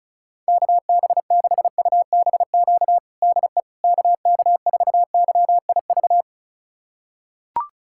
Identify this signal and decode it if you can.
Morse code, heard as a keyed tone, about 36 wpm, 690 Hz. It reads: KB6UBQ DE KK4YIV
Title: USA Callsign DE Callsign - 6 Characters - 35wpm